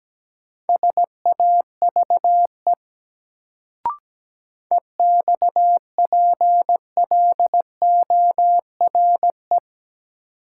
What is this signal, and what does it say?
Morse code, heard as a keyed tone, about 17 wpm, 700 Hz. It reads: SAVE EXPLORE
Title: Sentences from Top 1100 Words - New Word Drill - 17wpm